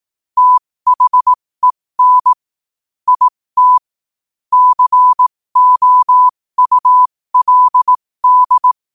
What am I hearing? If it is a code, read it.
Morse code, heard as a keyed tone, about 18 wpm, spaced as for 15 wpm, 1000 Hz. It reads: THEN IT COULD
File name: qotdfiles / 15wpm